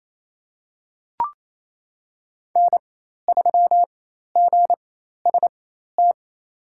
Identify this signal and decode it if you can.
Morse code, heard as a keyed tone, about 28 wpm, spaced as for 13 wpm, 700 Hz. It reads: N3GST
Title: USA Call Signs - 5 Characters - 4x Character Spacing - 28wpm